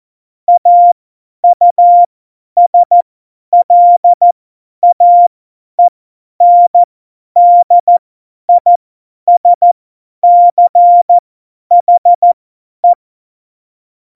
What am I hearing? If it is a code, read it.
Morse code, heard as a keyed tone, about 14 wpm, spaced as for 10 wpm, 700 Hz. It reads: AUSLAENDISCHE
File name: Morsekurs / Witze / Witz 29. August 17, Tempo 50 F70